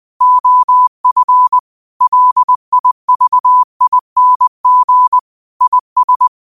Morse code, heard as a keyed tone, about 20 wpm, 1000 Hz. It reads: OF LIVING IS